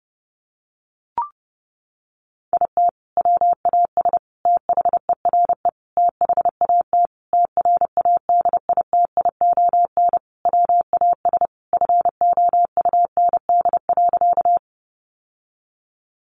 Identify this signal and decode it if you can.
Morse code, heard as a keyed tone, about 30 wpm, 700 Hz. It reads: IT WAS THERE THAT TRADITION WAS FOUND.
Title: Sentences from Top 1500 Words 30wpm